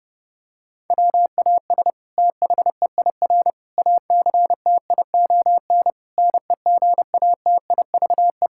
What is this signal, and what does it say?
Morse code, heard as a keyed tone, about 30 wpm, 700 Hz. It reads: WAS THEIR ACTION NEGATIVE
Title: Sentences from Top 1000 Words - Review Part 6 - 30wpm